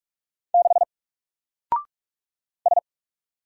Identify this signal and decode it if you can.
Morse code, heard as a keyed tone, about 44 wpm, 700 Hz. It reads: 6 S
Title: Single Letter-Number 45wpm